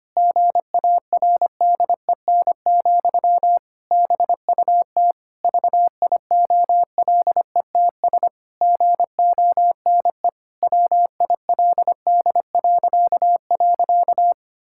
Morse code, heard as a keyed tone, about 25 wpm, 700 Hz. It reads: GARDEN, BUT VIOLETS GONE WILD..
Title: Pangram Challenge v2 - 25wpm